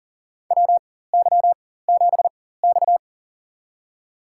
Morse code, heard as a keyed tone, about 40 wpm, spaced as for 19 wpm, 700 Hz. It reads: WY7X
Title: USA Call Signs - 4 Characters - 4x Character Spacing - 40wpm